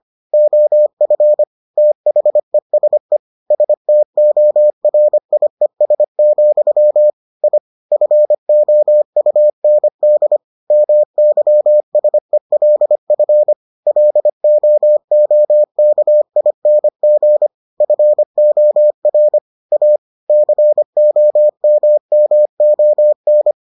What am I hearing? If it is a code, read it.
Morse code, heard as a keyed tone, about 25 wpm, 600 Hz. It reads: OF THESE STORIES, I FOUND MYSELF LOOKING FOR A COMMON